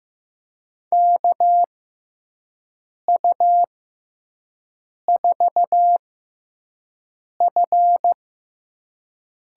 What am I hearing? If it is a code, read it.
Morse code, heard as a keyed tone, about 15 wpm, spaced as for 5 wpm, 700 Hz. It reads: KU4F
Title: USA Call Signs - 6x Character Spacing - 15wpm